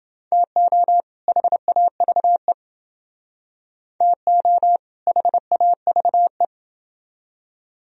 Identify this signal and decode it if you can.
Morse code, heard as a keyed tone, about 28 wpm, 700 Hz. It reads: TO HAVE TO HAVE